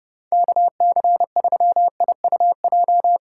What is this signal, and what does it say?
Morse code, heard as a keyed tone, about 29 wpm, 700 Hz. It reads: KC3IUJ
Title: USA Call Signs - 4x Character Spacing - 30wpm